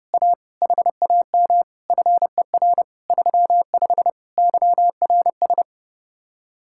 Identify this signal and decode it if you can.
Morse code, heard as a keyed tone, about 37 wpm, 700 Hz. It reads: A HAM FER 35 YRS
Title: Speed-Racing Common QSO Phrases - 30wpm